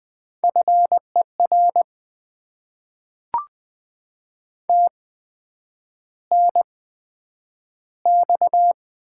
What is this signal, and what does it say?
Morse code, heard as a keyed tone, about 20 wpm, spaced as for 9 wpm, 700 Hz. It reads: FER TNX
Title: Top 20 QSO Elements - 8x Character Spacing - 20wpm